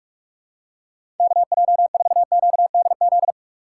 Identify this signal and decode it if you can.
Morse code, heard as a keyed tone, about 46 wpm, 700 Hz. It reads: KJ4QDZ
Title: USA Call Signs - 6 Characters - 4x Character Spacing - 45wpm